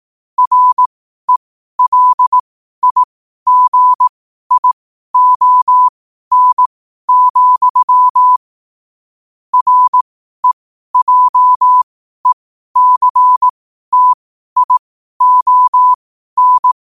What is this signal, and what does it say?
Morse code, heard as a keyed tone, about 18 wpm, spaced as for 12 wpm, 1000 Hz. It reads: RELIGION, REJECTION
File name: qotdfiles / 12wpm